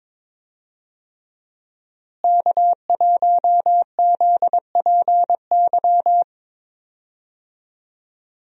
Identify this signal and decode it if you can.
Morse code, heard as a keyed tone, about 22 wpm, 700 Hz. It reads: K1ZPY
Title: USA Call Signs - 5 Characters - Repeated 3x - 22wpm